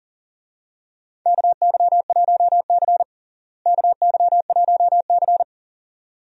Morse code, heard as a keyed tone, about 40 wpm, 700 Hz. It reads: KY1C KY1C
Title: USA Call Signs - 4 Characters - Repeated 2x - 40wpm